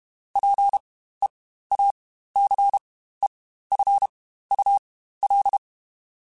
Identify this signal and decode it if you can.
Morse code, heard as a keyed tone, about 32 wpm, spaced as for 15 wpm, 800 Hz. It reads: PEACEFUL
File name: Quotes Spaced / Quotes Spaced-mp3s / ChildAsleep sp @30